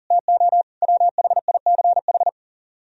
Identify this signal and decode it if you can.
Morse code, heard as a keyed tone, about 44 wpm, 700 Hz. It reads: TO WHICH